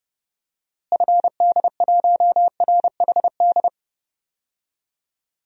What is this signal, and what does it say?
Morse code, heard as a keyed tone, about 29 wpm, 700 Hz. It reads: FD1RHD
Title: DXCC-100 Call Signs - 6 Characters - Repeated 2x - 30wpm